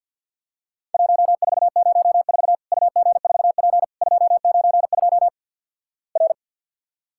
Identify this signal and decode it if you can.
Morse code, heard as a keyed tone, about 50 wpm, 700 Hz. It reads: 1404 UG4P 292 R
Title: Worked All Europe DX Contest - QTC Exchange - 50wpm